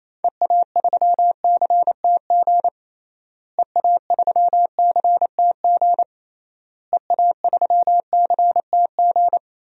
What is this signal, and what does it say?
Morse code, heard as a keyed tone, about 28 wpm, 700 Hz. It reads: EA3CTG EA3CTG EA3CTG
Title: DXCC-100 Call Signs - 6 Characters - Repeated 3x - 28wpm